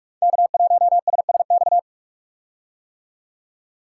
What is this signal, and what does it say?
Morse code, heard as a keyed tone, about 44 wpm, 690 Hz. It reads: K1SSX
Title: Code Groups - 5 Letter:Numbers - 45wpm